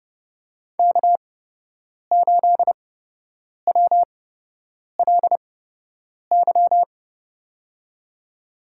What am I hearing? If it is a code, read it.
Morse code, heard as a keyed tone, about 30 wpm, spaced as for 8 wpm, 700 Hz. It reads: K8WLY
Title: USA Call Signs - 5 Characters - 8x Character Spacing - 30wpm